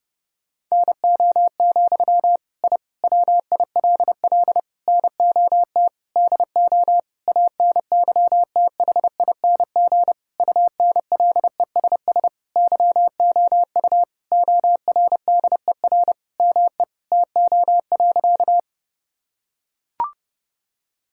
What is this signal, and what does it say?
Morse code, heard as a keyed tone, about 30 wpm, 700 Hz. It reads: NO, I WILL NOT DO ANYTHING UNLESS YOU ORDER ME TO.
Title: Sentences from Top 1000 Words - Review Part 6 - 30wpm